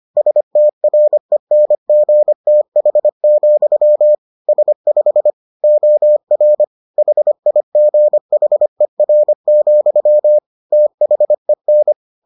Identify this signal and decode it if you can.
Morse code, heard as a keyed tone, about 25 wpm, 600 Hz. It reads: STRENGTH, S5 OR HIGHER, THEN